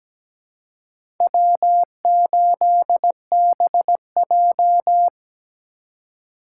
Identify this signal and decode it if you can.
Morse code, heard as a keyed tone, about 17 wpm, 700 Hz. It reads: W8BJ
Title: USA Call Signs - 4 Characters - 3x Character Spacing - 17wpm